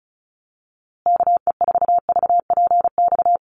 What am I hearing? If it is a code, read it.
Morse code, heard as a keyed tone, about 34 wpm, 700 Hz. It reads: KE4VPX
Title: USA Call Signs Repeated 2x 35wpm